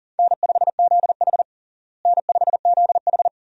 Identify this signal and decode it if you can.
Morse code, heard as a keyed tone, about 37 wpm, 700 Hz. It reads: N5ZH N5ZH
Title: USA Call Signs - 4 Characters - Repeated 2x - 40wpm